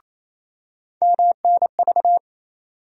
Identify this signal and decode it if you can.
Morse code, heard as a keyed tone, about 28 wpm, 700 Hz. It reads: MNV
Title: USA Call Sign Postfixes - 3 Letters 28wpm